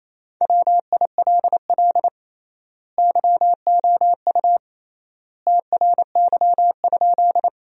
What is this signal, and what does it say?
Morse code, heard as a keyed tone, about 28 wpm, 700 Hz. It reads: WILL YOU TRY?
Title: 3x Word-Spacing - Sentences from Top 200 Words - 28wpm